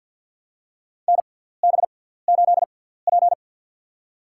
Morse code, heard as a keyed tone, about 51 wpm, spaced as for 17 wpm, 700 Hz. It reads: NB8P
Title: USA Call Signs - 4 Characters - 6x Character Spacing - 50wpm